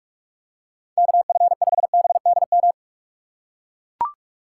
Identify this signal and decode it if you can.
Morse code, heard as a keyed tone, about 42 wpm, 700 Hz. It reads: KF5BDM
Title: USA Call Signs - 6 Characters - 3x Character Spacing - 45wpm